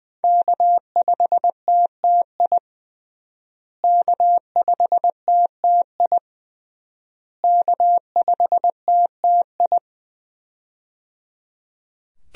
Morse code, heard as a keyed tone, about 20 wpm, 700 Hz. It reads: K5TTI K5TTI K5TTI
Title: USA Call Signs - 5 Characters - Repeated 3x - 20wpm